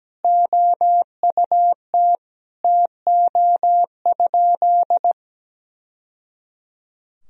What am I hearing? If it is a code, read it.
Morse code, heard as a keyed tone, about 17 wpm, 700 Hz. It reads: OUT TO?
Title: Sentences from Top 1000 Words - Review Part 3 - 17wpm